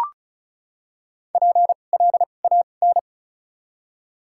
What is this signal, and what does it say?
Morse code, heard as a keyed tone, about 34 wpm, 700 Hz. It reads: P L A N
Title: 2x Character Spacing - Top 500 Words - 35wpm